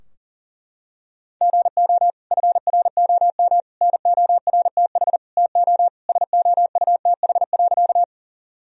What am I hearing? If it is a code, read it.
Morse code, heard as a keyed tone, about 37 wpm, 700 Hz. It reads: GO FROM NORTH TO SOUTH.
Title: Sentences from Top 1000 Words - Review Part 2 - 40wpm